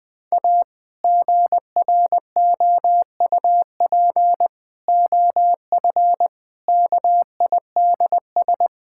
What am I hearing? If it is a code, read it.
Morse code, heard as a keyed tone, about 20 wpm, 700 Hz. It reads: A GROUP OF KIDS
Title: Sentences with 3-Word Phrases - 20wpm